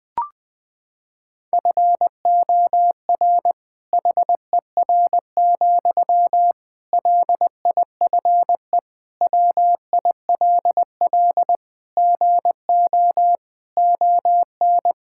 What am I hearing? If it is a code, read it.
Morse code, heard as a keyed tone, about 21 wpm, 700 Hz. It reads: FOR HER, LIFE WILL GO ON
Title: Sentences from Top 1600 Words 20wpm